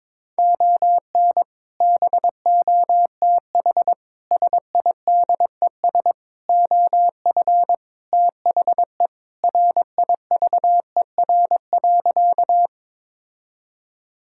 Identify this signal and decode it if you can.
Morse code, heard as a keyed tone, about 22 wpm, 700 Hz. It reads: ON BOTH SIDES OF THE RIVER.
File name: practice / 22wpm / Sentences with 4-Word Phrases - 22wpm